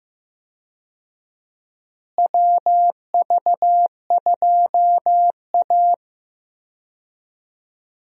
Morse code, heard as a keyed tone, about 15 wpm, 700 Hz. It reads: WV2A
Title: USA Call Signs - 4 Characters - Repeated 2x - 15wpm